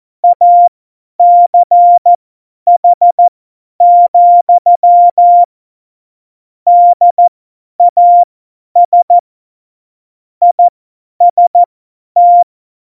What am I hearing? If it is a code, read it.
Morse code, heard as a keyed tone, about 14 wpm, spaced as for 10 wpm, 700 Hz. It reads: ACH, DAS IST
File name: Morsekurs / Witze / Witz 27. August 2019, 50 F70